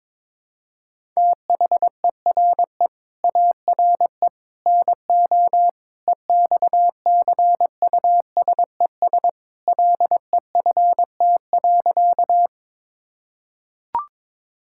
Morse code, heard as a keyed tone, about 22 wpm, 700 Hz. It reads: THERE ARE NO EXCUSES LEFT.
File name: practice / 22wpm / Sentences with 3-Word Phrases - 22wpm